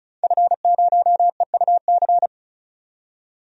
Morse code, heard as a keyed tone, about 36 wpm, 700 Hz. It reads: F0EUC
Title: DXCC-100 Call Signs - 5 Characters - 35wpm